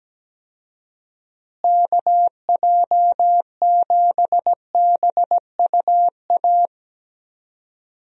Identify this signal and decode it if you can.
Morse code, heard as a keyed tone, about 17 wpm, 700 Hz. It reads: KJ7BUA
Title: USA Call Signs - 6 Characters - Repeated 3x - 17wpm